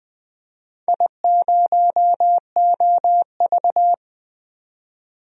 Morse code, heard as a keyed tone, about 20 wpm, 700 Hz. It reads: I0OV
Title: DXCC-100 Call Signs - 4 Characters - 20wpm